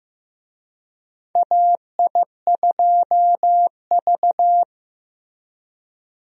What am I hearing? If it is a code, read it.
Morse code, heard as a keyed tone, about 15 wpm, 700 Hz. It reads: AI2V
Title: USA Call Signs - 4 Characters - Repeated 3x - 15wpm